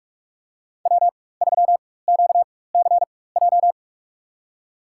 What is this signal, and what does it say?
Morse code, heard as a keyed tone, about 46 wpm, spaced as for 21 wpm, 700 Hz. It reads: W3QCJ
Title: USA Call Signs - 5 Characters - 4x Character Spacing - 45wpm